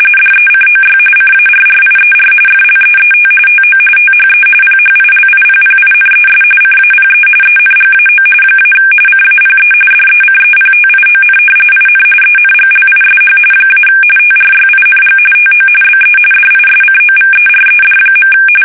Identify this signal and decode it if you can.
RTTY, about 45.45 baud. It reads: DECOY CARRIER FLEET TO THE PHILIPPINES CAPE ENGAO - CAPE DECEPTION IN ENGLISH - TRYING TO BE DISCOVERED.
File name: fsk850 / battleshipnj